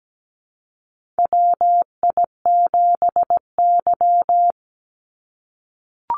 Morse code, heard as a keyed tone, about 17 wpm, 700 Hz. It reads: WI7Y
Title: USA Call Signs 17wpm